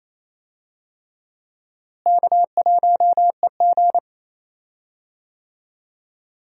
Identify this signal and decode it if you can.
Morse code, heard as a keyed tone, about 27 wpm, 700 Hz. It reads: K1EG
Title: USA Call Signs - 4 Characters - Repeated 3x - 28wpm